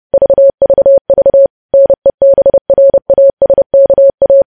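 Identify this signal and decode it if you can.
Morse code, heard as a keyed tone, about 30 wpm, 550 Hz. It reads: VVV NEBRASKA
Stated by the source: News headlines sent in Morse code at 30 wpm.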